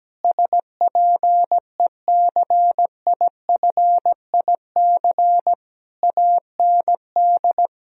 Morse code, heard as a keyed tone, about 17 wpm, 700 Hz. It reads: SPECIFIC AND
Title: Sentences from Top 1000 Words - Review Part 3 - 17wpm